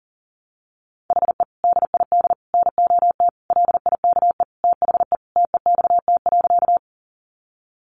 Morse code, heard as a keyed tone, about 40 wpm, 700 Hz. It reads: HE DID NOT LIKE THE TEXT.
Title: Sentences from Top 1200 Words 40wpm